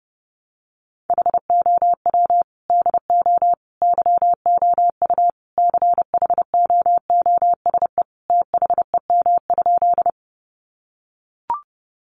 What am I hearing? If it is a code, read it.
Morse code, heard as a keyed tone, about 30 wpm, 700 Hz. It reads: HOW DO YOU CHOOSE THEM?
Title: Sentences from Top 700 Words 30wpm